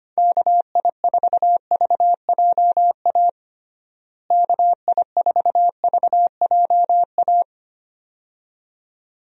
Morse code, heard as a keyed tone, about 25 wpm, 700 Hz. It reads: KI4VJA KI4VJA
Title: USA Call Signs - 6 Characters - Repeated 2x - 25wpm